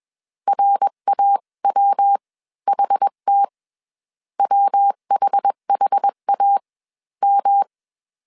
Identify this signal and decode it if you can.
Morse code, heard as a keyed tone, about 21 wpm, spaced as for 16 wpm, 800 Hz. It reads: RAWHT WHHAM